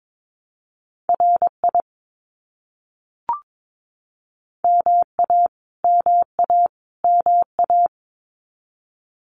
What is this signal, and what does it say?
Morse code, heard as a keyed tone, about 22 wpm, 700 Hz. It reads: RI MA MA MA
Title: US States Repeated 3x 22wpm